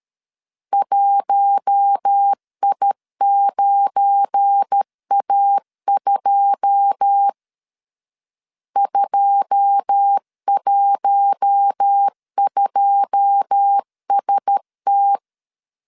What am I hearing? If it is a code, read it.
Morse code, heard as a keyed tone, about 13 wpm, 800 Hz. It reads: 1I9A2 212ST
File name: LEZIONI / 9 2 12wpm 2301